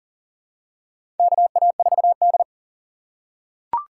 Morse code, heard as a keyed tone, about 40 wpm, 700 Hz. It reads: KA4D
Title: USA Call Signs - 4 Characters - 2x Character Spacing - 40wpm